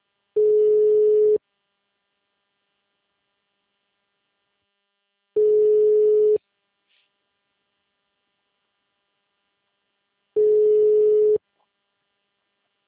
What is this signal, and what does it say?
Ringback tone (Europe, 425 Hz).